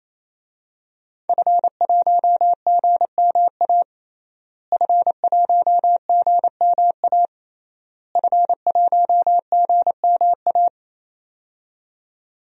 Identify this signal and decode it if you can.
Morse code, heard as a keyed tone, about 28 wpm, 700 Hz. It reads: F1GMA F1GMA F1GMA
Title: DXCC-100 Call Signs - 5 Characters - Repeated 3x - 28wpm